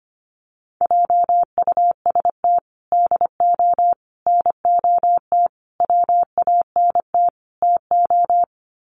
Morse code, heard as a keyed tone, about 25 wpm, 700 Hz. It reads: JUST DO NOT WANT TO
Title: Sets of 5 Words Encore 25wpm